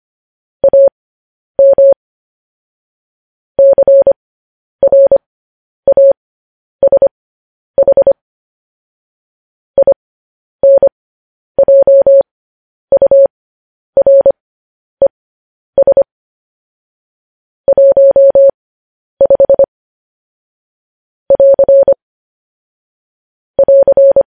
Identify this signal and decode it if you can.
Morse code, heard as a keyed tone, about 25 wpm, spaced as for 10 wpm, 550 Hz. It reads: AM CRASH INJURES 15 <AR> <AR>